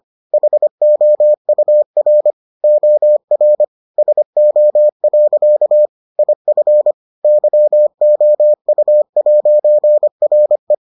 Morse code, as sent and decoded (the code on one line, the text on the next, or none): .... --- ..- .-. / --- .-. / ... --- .-.-.- / .. ..-. / -.-- --- ..- .----. .-. .
HOUR OR SO. IF YOU'RE